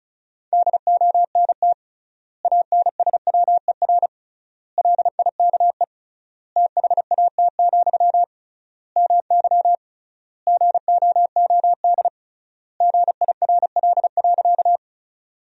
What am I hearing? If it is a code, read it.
Morse code, heard as a keyed tone, about 36 wpm, 700 Hz. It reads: DONT ANSWER LIKE THAT, MY GOOD GIRL.
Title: 3x Word-Spacing - Sentences from Top 400 Words - 35wpm